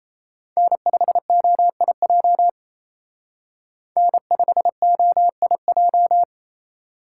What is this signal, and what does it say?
Morse code, heard as a keyed tone, about 32 wpm, 700 Hz. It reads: N5OIJ N5OIJ
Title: Speed Racing - USA 1x3 Call Signs 28wpm